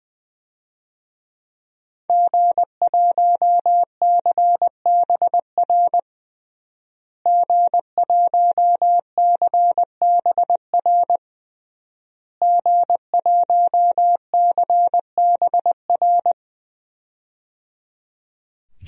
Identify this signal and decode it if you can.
Morse code, heard as a keyed tone, about 21 wpm, 700 Hz. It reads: G1CBR G1CBR G1CBR
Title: DXCC-100 Call Signs - 5 Characters - Repeated 3x - 20wpm